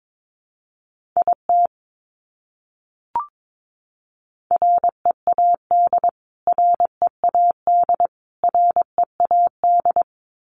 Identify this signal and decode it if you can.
Morse code, heard as a keyed tone, about 22 wpm, 700 Hz. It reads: IT READ READ READ
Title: Top 200 Words Repeated 22wpm